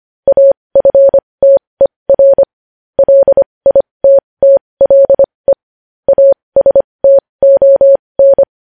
Morse code, heard as a keyed tone, about 25 wpm, spaced as for 20 wpm, 550 Hz. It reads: AFTER LITTLE ASTON
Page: news-headlines-in-morse-code-at-20-wpm